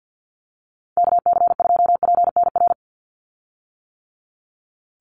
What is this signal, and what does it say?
Morse code, heard as a keyed tone, about 50 wpm, 700 Hz. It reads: KC2PNR
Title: USA Call Signs 50wpm